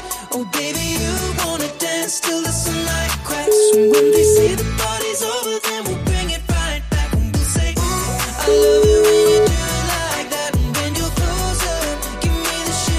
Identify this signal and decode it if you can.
Ringback tone (Europe, 425 Hz), with noise.